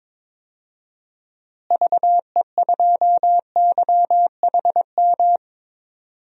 Morse code, heard as a keyed tone, about 22 wpm, 700 Hz. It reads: VE2YHM
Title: DXCC-100 Call Signs - 6 Characters - Repeated 2x - 22wpm